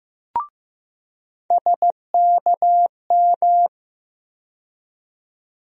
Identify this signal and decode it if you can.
Morse code, heard as a keyed tone, about 15 wpm, 700 Hz. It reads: SKM
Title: USA Call Sign Postfixes - 3 Letters 15wpm